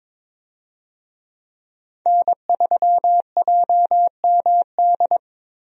Morse code, heard as a keyed tone, about 22 wpm, 700 Hz. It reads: N3JMD
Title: USA Call Signs - 5 Characters - 22wpm